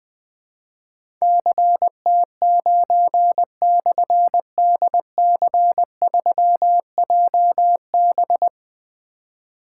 Morse code, heard as a keyed tone, about 20 wpm, 700 Hz. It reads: CT9/DC3JB
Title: Difficult Call Signs 20wpm